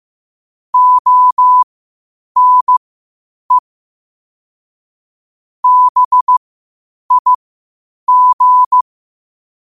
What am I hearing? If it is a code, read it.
Morse code, heard as a keyed tone, about 15 wpm, spaced as for 8 wpm, 1000 Hz. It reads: ONE BIG
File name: 8wpm